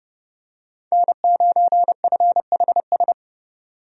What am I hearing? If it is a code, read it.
Morse code, heard as a keyed tone, about 30 wpm, 700 Hz. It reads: N9FHS
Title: USA Call Signs - 5 Characters - 30wpm